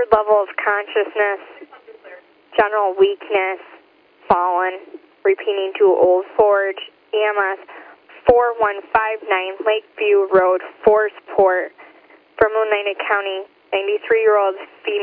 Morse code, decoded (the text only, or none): none